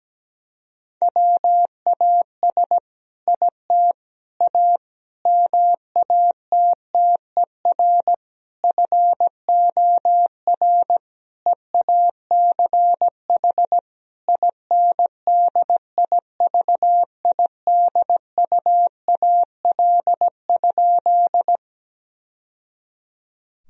Morse code, heard as a keyed tone, about 17 wpm, 700 Hz. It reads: WAS IT A MATTER FOR EACH INDIVIDUAL?
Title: Sentences from Top 1000 Words - Review Part 4 - 17wpm